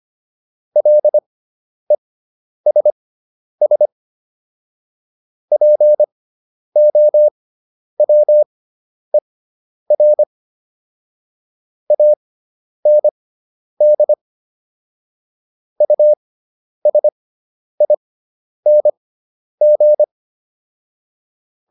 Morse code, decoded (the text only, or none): LESS POWER AND USING